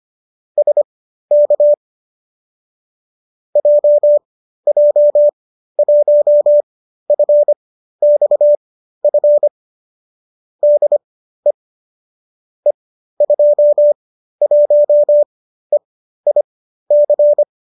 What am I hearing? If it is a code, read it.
Morse code, heard as a keyed tone, about 25 wpm, spaced as for 13 wpm, 590 Hz. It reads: SK JJ1FXF DE E21EIC